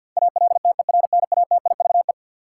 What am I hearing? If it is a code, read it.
Morse code, heard as a keyed tone, about 53 wpm, 700 Hz. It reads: ALTERNATIVE
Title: Sentences from Top 1000 Words - Review Part 4 - New Word Drill - 50wpm